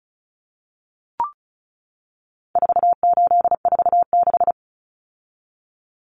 Morse code, heard as a keyed tone, about 34 wpm, 700 Hz. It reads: 4846
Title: Common Cut Numbers - 4-Digit 35wpm